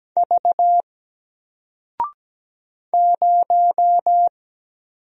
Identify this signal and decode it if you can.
Morse code, heard as a keyed tone, about 17 wpm, 700 Hz. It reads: V 0
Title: Single Letter-Number 17wpm